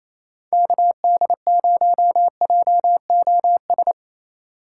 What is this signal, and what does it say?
Morse code, heard as a keyed tone, about 27 wpm, 700 Hz. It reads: KD0JOS
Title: USA Call Signs - 4x Character Spacing - 28wpm